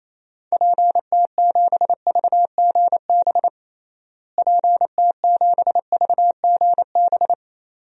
Morse code, heard as a keyed tone, about 28 wpm, 700 Hz. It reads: PT7VGB PT7VGB
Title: DXCC-100 Call Signs - 6 Characters - Repeated 2x - 28wpm